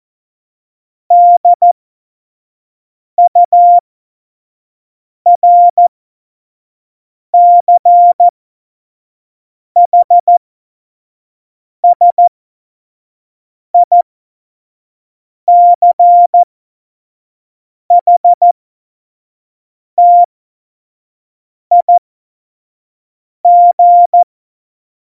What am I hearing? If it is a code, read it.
Morse code, heard as a keyed tone, about 14 wpm, spaced as for 5 wpm, 700 Hz. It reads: DURCHSICHTIG
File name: Morsekurs / Witze / Witz 14. November 2017, 25 F70